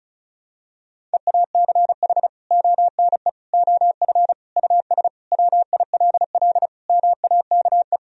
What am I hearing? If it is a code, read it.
Morse code, heard as a keyed tone, about 36 wpm, 700 Hz. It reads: EACH ONE OF US WILL MAKE
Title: Sentences from Top 1000 Words - Review Part 2 - 35wpm